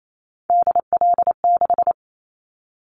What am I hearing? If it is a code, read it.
Morse code, heard as a keyed tone, about 28 wpm, 700 Hz. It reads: DL6
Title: Callsign Prefixes for Top 25 DXCC - 28wpm